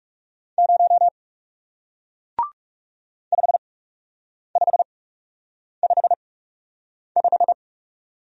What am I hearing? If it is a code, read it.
Morse code, heard as a keyed tone, about 40 wpm, spaced as for 8 wpm, 700 Hz. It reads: 0 5555